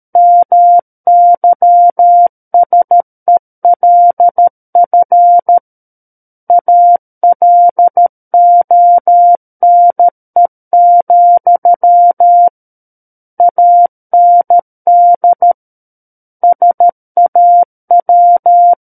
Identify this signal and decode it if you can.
Morse code, heard as a keyed tone, about 13 wpm, 700 Hz. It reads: MYSELF ALONE, AND SAW
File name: Book One / 04-Chapter 4 - 13 WPM